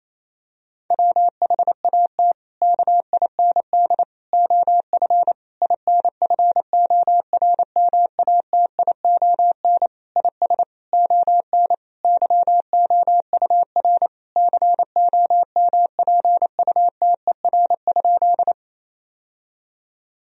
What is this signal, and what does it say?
Morse code, heard as a keyed tone, about 27 wpm, 700 Hz. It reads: WHAT KIND OF INFORMATION IS ON YOUR COMPUTER?
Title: Sentences from Top 1000 Words - Review Part 3 - 28wpm